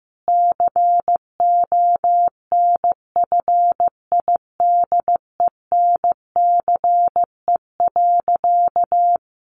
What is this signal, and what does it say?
Morse code, heard as a keyed tone, about 15 wpm, 700 Hz. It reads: CONFIDENCE.
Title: Sentences from Top 1500 Words 15wpm